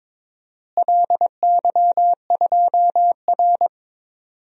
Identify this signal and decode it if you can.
Morse code, heard as a keyed tone, about 22 wpm, 700 Hz. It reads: LY2R
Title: DXCC-100 Call Signs - 4 Characters - Repeated 2x - 22wpm